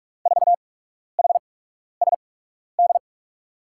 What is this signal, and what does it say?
Morse code, heard as a keyed tone, about 41 wpm, spaced as for 12 wpm, 700 Hz. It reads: 4HSD
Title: USA Call Signs - 6 Characters - 8x Character Spacing - 45wpm